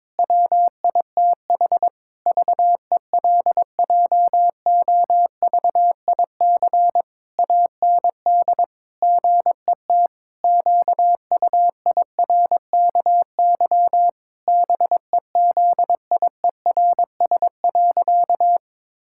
Morse code, heard as a keyed tone, about 22 wpm, 700 Hz. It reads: WITH VELJOVIC AND GET QUIRKY BEZIERS.
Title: Pangram Challenge v2 - 22wpm